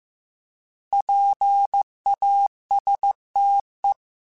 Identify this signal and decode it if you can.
Morse code, heard as a keyed tone, about 15 wpm, 800 Hz. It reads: PASTE
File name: Alphabetical / Alphabetical mp3s / Ps @13